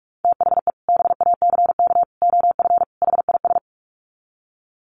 Morse code, heard as a keyed tone, about 45 wpm, 700 Hz. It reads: THE BACK OF HIS
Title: Sets of 4 Words 45wpm